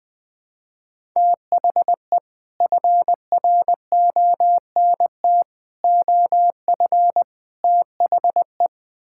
Morse code, heard as a keyed tone, about 20 wpm, 700 Hz. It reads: THE FRONT OF THE
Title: Build A Phrase - 4-Word Phrases - 20wpm